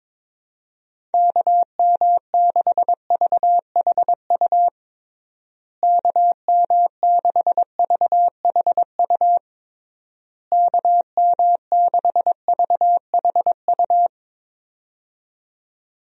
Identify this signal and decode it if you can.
Morse code, heard as a keyed tone, about 22 wpm, 700 Hz. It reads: KM6VHU KM6VHU KM6VHU
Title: USA Call Signs - 6 Characters - Repeated 3x - 22wpm